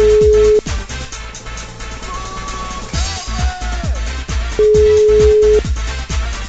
Ringback tone (Europe, 425 Hz).